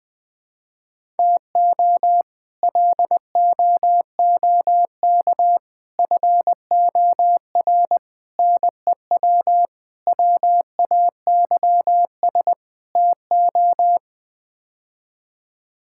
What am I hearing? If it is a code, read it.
Morse code, heard as a keyed tone, about 20 wpm, 700 Hz. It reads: TO LOOK FOR NEW WAYS TO
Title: Sets of 6 Words - 20wpm